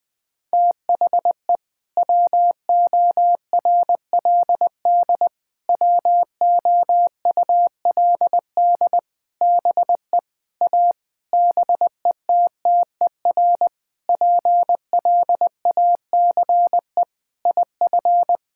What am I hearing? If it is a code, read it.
Morse code, heard as a keyed tone, about 20 wpm, 700 Hz. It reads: THE WORLD WOULD BE A BETTER PLACE IF
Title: Sentences with 8-Word Phrases - 20wpm